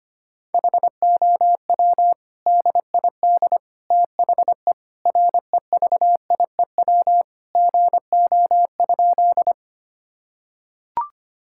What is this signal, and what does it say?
Morse code, heard as a keyed tone, about 25 wpm, 700 Hz. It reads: HOW DID THE REVIEW GO?
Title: Sentences from Top 1000 Words - Review Part 6 - 25wpm